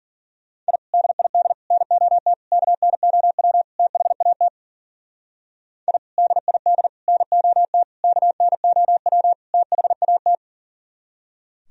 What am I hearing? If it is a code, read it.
Morse code, heard as a keyed tone, about 52 wpm, 700 Hz. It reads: I DID NOT KNOW THAT I DID NOT KNOW THAT